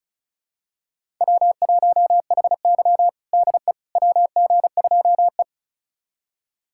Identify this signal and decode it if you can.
Morse code, heard as a keyed tone, about 36 wpm, 700 Hz. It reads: W1HY DE WG2E
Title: USA Callsign DE Callsign - 4 Characters - 35wpm